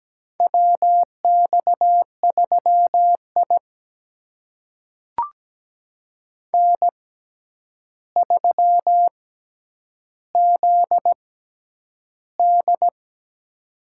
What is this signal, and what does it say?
Morse code, heard as a keyed tone, about 17 wpm, spaced as for 9 wpm, 700 Hz. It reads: WX3I N3ZD
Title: USA Call Signs - 4 Characters - 6x Character Spacing - 17wpm